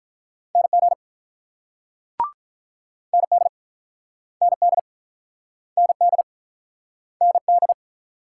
Morse code, heard as a keyed tone, about 45 wpm, 700 Hz. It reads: NG ND ND ND ND